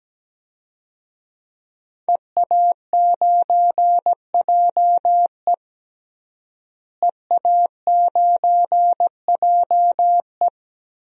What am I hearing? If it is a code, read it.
Morse code, heard as a keyed tone, about 17 wpm, 700 Hz. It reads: EA9JE EA9JE
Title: DXCC-100 Call Signs - 5 Characters - Repeated 2x - 17wpm